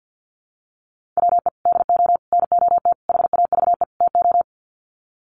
Morse code, heard as a keyed tone, about 50 wpm, 700 Hz. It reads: WE DO NOT HAVE TO
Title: Sets of 5 Words 50wpm